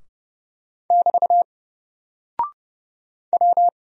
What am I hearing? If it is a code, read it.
Morse code, heard as a keyed tone, about 31 wpm, 700 Hz. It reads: <BT> W
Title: Single Character (Letter:Number:Punctuation:Prosign) 30wpm